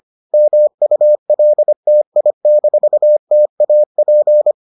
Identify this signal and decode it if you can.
Morse code, heard as a keyed tone, about 25 wpm, 600 Hz. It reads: MULTI-TAP